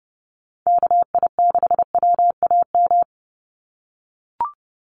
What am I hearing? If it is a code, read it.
Morse code, heard as a keyed tone, about 30 wpm, 700 Hz. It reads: KI6WAM
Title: USA Call Signs 30wpm